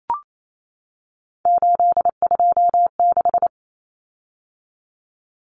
Morse code, heard as a keyed tone, about 28 wpm, 700 Hz. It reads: 826
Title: Common Cut Numbers - 3-Digit 28wpm